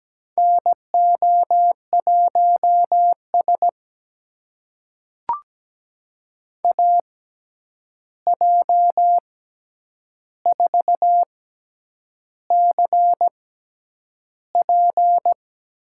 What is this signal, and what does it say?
Morse code, heard as a keyed tone, about 17 wpm, spaced as for 8 wpm, 700 Hz. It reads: NO1S AJ4CP
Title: USA Call Signs - 6x Character Spacing - 17wpm